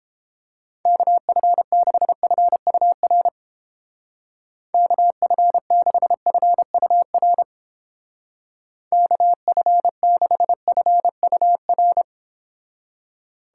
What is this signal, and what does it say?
Morse code, heard as a keyed tone, about 29 wpm, 700 Hz. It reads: KF6FUR KF6FUR KF6FUR